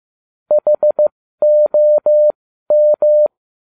Morse code, heard as a keyed tone, about 15 wpm, spaced as for 12 wpm, 600 Hz. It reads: HOM